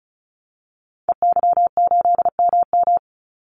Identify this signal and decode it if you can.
Morse code, heard as a keyed tone, about 34 wpm, 700 Hz. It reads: EY8MM
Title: Call Signs Repeated 2x - 35wpm